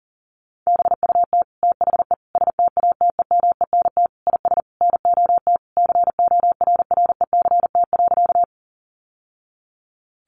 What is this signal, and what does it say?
Morse code, heard as a keyed tone, about 40 wpm, 700 Hz. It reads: BUT THE STATEMENT IS NOT CORRECT.
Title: Sentences from Top 1200 Words 40wpm